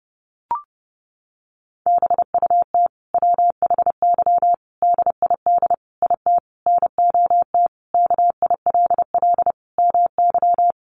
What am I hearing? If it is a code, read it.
Morse code, heard as a keyed tone, about 30 wpm, 700 Hz. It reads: BUT WHY DID IT NOT KILL MY
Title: Sentences from Top 1400 Words 30wpm